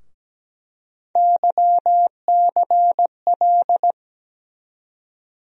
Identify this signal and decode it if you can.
Morse code, heard as a keyed tone, about 17 wpm, 700 Hz. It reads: YCL
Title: USA Call Sign Postfixes - 3 Letters 17wpm